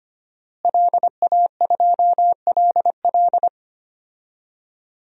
Morse code, heard as a keyed tone, about 25 wpm, 700 Hz. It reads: LA2LL
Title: DXCC-100 Call Signs - 5 Characters - Repeated 2x - 25wpm